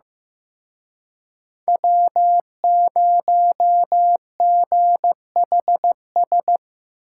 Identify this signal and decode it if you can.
Morse code, heard as a keyed tone, about 17 wpm, 700 Hz. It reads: W0GHS
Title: Speed Racing - USA Call Signs 15wpm